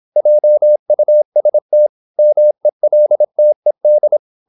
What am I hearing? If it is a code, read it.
Morse code, heard as a keyed tone, about 26 wpm, 600 Hz. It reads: JUST MELTED
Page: Never 26wpm | CW med Gnister